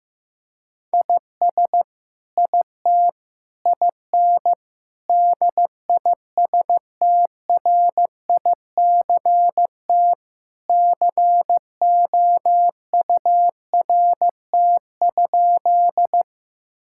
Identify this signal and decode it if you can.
Morse code, heard as a keyed tone, about 15 wpm, 700 Hz. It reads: IS IT IN DISTRICT COURT?
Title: Sentences from Top 1600 Words 15wpm